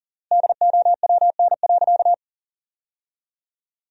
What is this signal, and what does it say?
Morse code, heard as a keyed tone, about 40 wpm, 700 Hz. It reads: DOWN.
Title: 3x Word-Spacing - Sentences from Top 500 Words - 40wpm